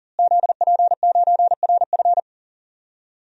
Morse code, heard as a keyed tone, about 40 wpm, 700 Hz. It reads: ZP9RF
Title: DXCC-100 Call Signs - 5 Characters - Repeated 3x - 40wpm